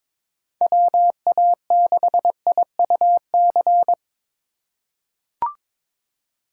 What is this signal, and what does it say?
Morse code, heard as a keyed tone, about 22 wpm, 700 Hz. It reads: WA6IUC
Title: USA Call Signs - 6 Characters - 3x Character Spacing - 22wpm